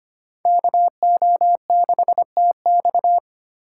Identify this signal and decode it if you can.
Morse code, heard as a keyed tone, about 33 wpm, 700 Hz. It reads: KO6TX